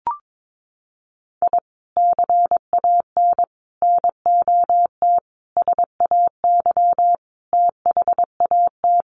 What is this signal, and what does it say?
Morse code, heard as a keyed tone, about 22 wpm, 700 Hz. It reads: I CAN NOT SAY THAT
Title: Sets of 5 Words Encore2 22wpm